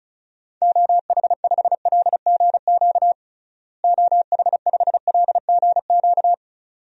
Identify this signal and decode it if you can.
Morse code, heard as a keyed tone, about 34 wpm, 700 Hz. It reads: OH5LGQ OH5LGQ
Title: DXCC-100 Call Signs - 6 Characters - Repeated 2x - 35wpm